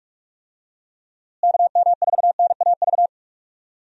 Morse code, heard as a keyed tone, about 46 wpm, 700 Hz. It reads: KM4NAV
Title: USA Call Signs - 6 Characters - Repeated 3x - 45wpm